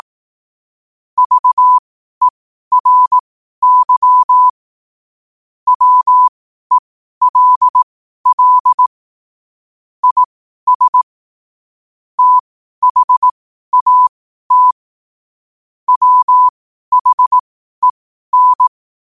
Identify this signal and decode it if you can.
Morse code, heard as a keyed tone, about 18 wpm, spaced as for 12 wpm, 1000 Hz. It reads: VERY WELL IS THAT WHEN
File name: qotdfiles / 12wpm